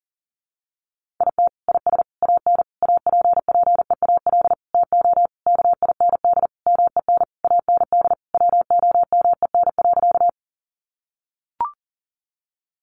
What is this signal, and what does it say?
Morse code, heard as a keyed tone, about 40 wpm, 700 Hz. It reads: IT IS AN APPEAL TO KIND MEN AND WOMEN.
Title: Sentences from Top 1300 Words 40wpm